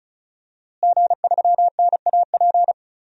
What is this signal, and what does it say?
Morse code, heard as a keyed tone, about 36 wpm, 700 Hz. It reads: G3NAP
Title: DXCC-100 Call Signs - 5 Characters - Repeated 3x - 35wpm